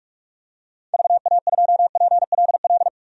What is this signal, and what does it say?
Morse code, heard as a keyed tone, about 45 wpm, 700 Hz. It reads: VA2PLL
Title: DXCC-100 Call Signs - 6 Characters - 45wpm